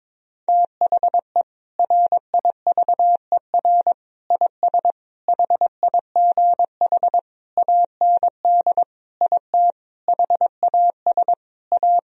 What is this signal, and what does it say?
Morse code, heard as a keyed tone, about 22 wpm, 700 Hz. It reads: THE RIVER IS HIGH AND IT HAS A
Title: Sentences from Top 1000 Words - Review Part 4 - 22wpm